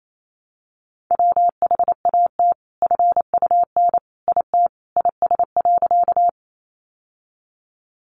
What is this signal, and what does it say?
Morse code, heard as a keyed tone, about 28 wpm, 700 Hz. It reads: WHAT FUN IT IS.
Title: Sentences from Top 700 Words 28wpm